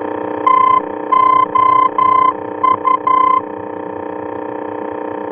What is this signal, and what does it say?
Morse code, heard as a keyed tone, about 11 wpm, 1030 Hz, with noise. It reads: TOU
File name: france / mx / TOU-F-VOR